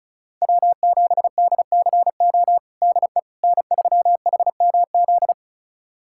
Morse code, heard as a keyed tone, about 34 wpm, 690 Hz. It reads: W7DCO DE N3HMZ
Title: USA Callsign DE Callsign - 5 Characters - 35wpm